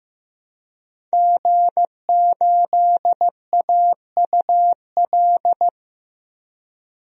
Morse code, heard as a keyed tone, about 15 wpm, 700 Hz. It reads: G8AUL
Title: DXCC-100 Call Signs - 5 Characters - Repeated 2x - 15wpm